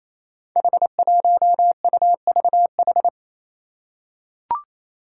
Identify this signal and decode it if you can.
Morse code, heard as a keyed tone, about 27 wpm, 700 Hz. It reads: H1UVH
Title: Five Challenging Letter-Number Combinations 28wpm